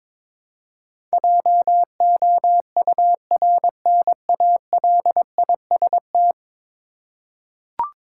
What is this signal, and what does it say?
Morse code, heard as a keyed tone, about 22 wpm, 700 Hz. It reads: JOURNALIST
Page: morse-code-ninja-practice-22wpm